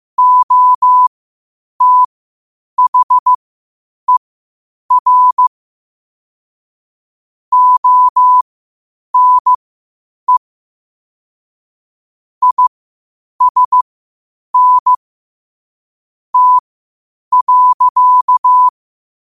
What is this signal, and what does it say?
Morse code, heard as a keyed tone, about 15 wpm, spaced as for 8 wpm, 1000 Hz. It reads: OTHER ONE ISN T.
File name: qotdfiles / 8wpm